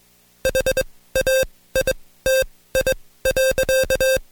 Morse code, heard as a keyed tone, about 21 wpm, spaced as for 16 wpm, 530 Hz. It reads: HAITI.
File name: Code Quick / AudioDownld2 / Seg134-Stepper7 15WPM